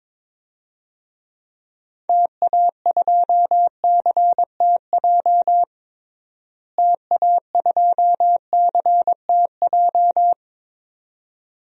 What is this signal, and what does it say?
Morse code, heard as a keyed tone, about 22 wpm, 700 Hz. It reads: TA2CTJ TA2CTJ
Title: DXCC-100 Call Signs - 6 Characters - Repeated 2x - 22wpm